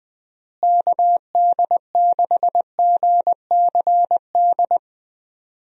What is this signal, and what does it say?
Morse code, heard as a keyed tone, about 21 wpm, 700 Hz. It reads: KD6GCD
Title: USA Call Signs - 6 Characters - Repeated 2x - 20wpm